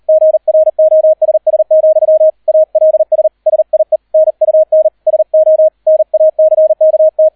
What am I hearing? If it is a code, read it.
Morse code, heard as a keyed tone, about 38 wpm, 630 Hz. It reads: GROSS, ALS SIE NUN SO NACKT